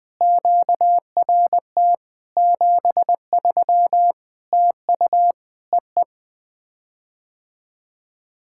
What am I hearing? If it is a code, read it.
Morse code, heard as a keyed tone, about 20 wpm, 700 Hz. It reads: QRT 73 TU EE
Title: POTA QSO Practice - 20wpm